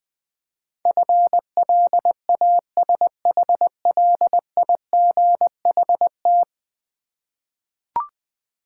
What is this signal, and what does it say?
Morse code, heard as a keyed tone, about 20 wpm, 700 Hz. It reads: FLASHLIGHT
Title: Easy Words with Hints - 20wpm